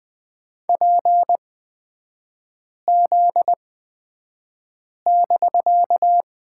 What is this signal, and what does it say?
Morse code, heard as a keyed tone, about 20 wpm, spaced as for 5 wpm, 700 Hz. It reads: PZ<BK>
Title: 250 Lesson 40e - BK - Three Characters 20wpm